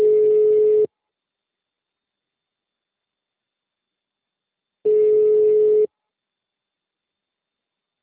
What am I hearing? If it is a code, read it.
Ringback tone (Europe, 425 Hz).
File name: out- -2002- - - .1227